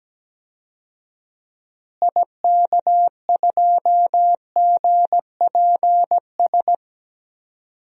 Morse code, heard as a keyed tone, about 17 wpm, 700 Hz. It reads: IK2GPS
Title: DXCC-100 Call Signs - 6 Characters - Repeated 2x - 17wpm